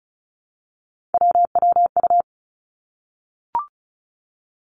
Morse code, heard as a keyed tone, about 36 wpm, 700 Hz. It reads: WWU
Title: Three Challenging Letter-Number Combinations 35wpm